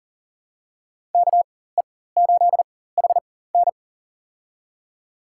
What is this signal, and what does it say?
Morse code, heard as a keyed tone, about 40 wpm, spaced as for 19 wpm, 700 Hz. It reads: KE8HN
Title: USA Call Signs - 4x Character Spacing - 40wpm